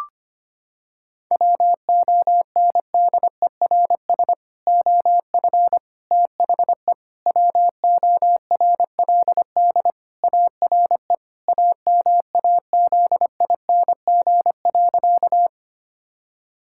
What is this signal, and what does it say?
Morse code, heard as a keyed tone, about 25 wpm, 700 Hz. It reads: WONDERS OF THE WORLD ARE AMAZING.
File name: practice / 25wpm / Sentences with 3-Word Phrases - 25wpm